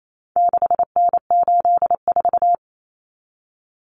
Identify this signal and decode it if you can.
Morse code, heard as a keyed tone, about 28 wpm, 700 Hz. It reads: 6N84
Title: Common Cut Numbers - 4-Digit 28wpm